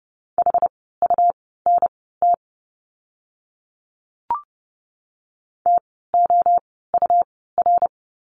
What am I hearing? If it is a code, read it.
Morse code, heard as a keyed tone, about 30 wpm, spaced as for 17 wpm, 700 Hz. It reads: HUNT TOUR
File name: Lesson 17f - Letter u - Words 30wpm